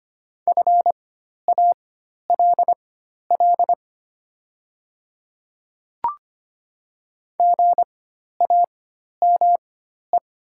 Morse code, heard as a keyed tone, about 25 wpm, spaced as for 12 wpm, 700 Hz. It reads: FALL GAME
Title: 4x Character Spacing - Top 500 Words - 25wpm